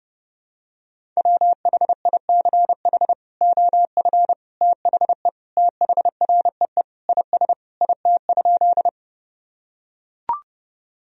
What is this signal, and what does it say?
Morse code, heard as a keyed tone, about 30 wpm, 700 Hz. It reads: WHICH OF THE THREE IS IT?
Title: Sentences from Top 1000 Words - Review Part 1 - 30wpm